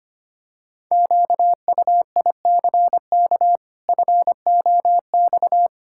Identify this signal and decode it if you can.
Morse code, heard as a keyed tone, about 25 wpm, 700 Hz. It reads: QUICK FOX
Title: Pangram Challenge v2 - 25wpm